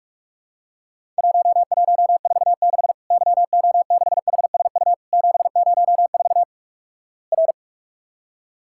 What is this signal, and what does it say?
Morse code, heard as a keyed tone, about 46 wpm, 700 Hz. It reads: 1146 YO6HSU 704 R
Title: Worked All Europe DX Contest - QTC Exchange - 45wpm